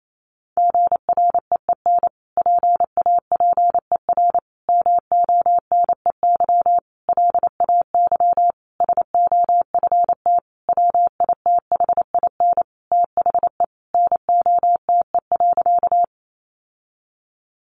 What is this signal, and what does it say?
Morse code, heard as a keyed tone, about 28 wpm, 700 Hz. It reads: GREEN PAPER MONEY LAY SOFT WITHIN THE NOTE.
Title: Sentences from Top 1500 Words 28wpm